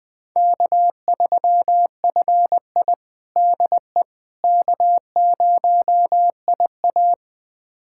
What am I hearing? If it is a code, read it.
Morse code, heard as a keyed tone, about 21 wpm, 700 Hz. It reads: K3FI DE K0IA
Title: USA Callsign DE Callsign - 4 Characters - 20wpm